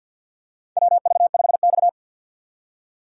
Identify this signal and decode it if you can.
Morse code, heard as a keyed tone, about 50 wpm, 700 Hz. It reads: WV5X
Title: USA Call Signs - 4 Characters - 3x Character Spacing - 50wpm